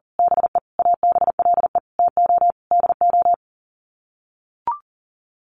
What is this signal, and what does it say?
Morse code, heard as a keyed tone, about 40 wpm, 700 Hz. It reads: BE ABLE TO DO
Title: Sets of 5 Words 40wpm